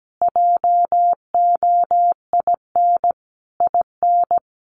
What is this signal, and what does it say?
Morse code, heard as a keyed tone, about 17 wpm, 700 Hz. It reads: JOIN IN
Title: Sentences from Top 1400 Words 17wpm